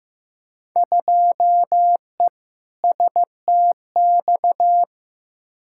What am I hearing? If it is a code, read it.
Morse code, heard as a keyed tone, about 15 wpm, 700 Hz. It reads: 2E STX
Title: Field Day Exchanges Encore - 15wpm